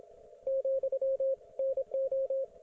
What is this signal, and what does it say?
Morse code, heard as a keyed tone, about 27 wpm, 540 Hz, with radio noise. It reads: , NO